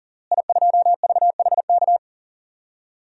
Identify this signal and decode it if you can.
Morse code, heard as a keyed tone, about 40 wpm, 700 Hz. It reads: I2VHK
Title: DXCC-100 Call Signs - 5 Characters - Repeated 3x - 40wpm